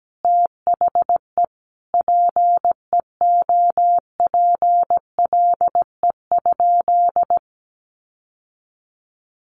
Morse code, heard as a keyed tone, about 17 wpm, 700 Hz. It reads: THE PEOPLE?
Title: Sentences from Top 1500 Words 17wpm